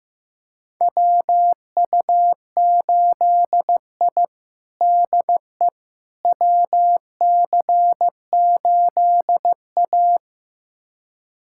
Morse code, heard as a keyed tone, about 15 wpm, 700 Hz. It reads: WU8I DE WC8A
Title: USA Callsign DE Callsign - 4 Characters - 15wpm